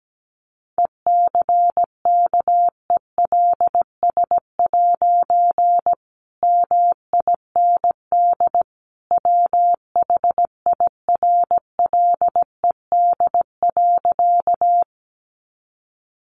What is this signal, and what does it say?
Morse code, heard as a keyed tone, about 17 wpm, 700 Hz. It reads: ECKELS' MIND WHIRLED.
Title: A Sound Of Thunder 17wpm